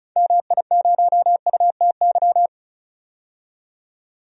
Morse code, heard as a keyed tone, about 34 wpm, 700 Hz. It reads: MI0UTY
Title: DXCC-100 Call Signs - 6 Characters - 35wpm